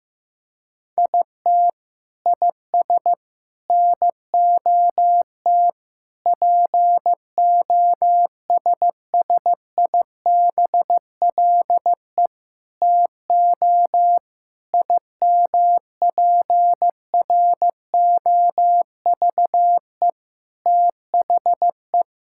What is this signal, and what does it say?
Morse code, heard as a keyed tone, about 16 wpm, 700 Hz. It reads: IT IS NOT POSSIBLE TO IMPROVE THE
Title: Sentences from Top 1000 Words - Review Part 5 - 15wpm